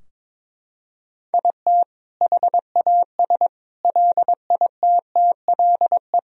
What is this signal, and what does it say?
Morse code, heard as a keyed tone, about 22 wpm, 700 Hz. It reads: IT HAS LITTLE
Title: Sentences from Top 1000 Words - Review Part 4 - 22wpm